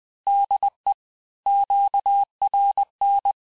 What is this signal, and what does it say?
Morse code, heard as a keyed tone, about 20 wpm, 800 Hz. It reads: DE QRN
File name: cours-du-lundi files / cours-1584-Fin-de-QSO-vitesse-19mpm